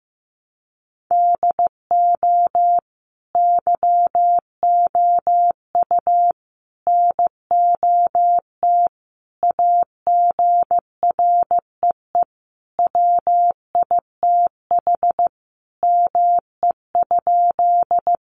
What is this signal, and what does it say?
Morse code, heard as a keyed tone, about 15 wpm, 700 Hz. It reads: DO YOU NOT AGREE WITH ME?
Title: Sentences from Top 1100 Words 15wpm